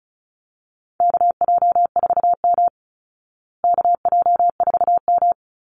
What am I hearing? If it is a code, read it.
Morse code, heard as a keyed tone, about 34 wpm, 700 Hz. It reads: KJ4M KJ4M
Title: Call Signs Repeated 2x - 35wpm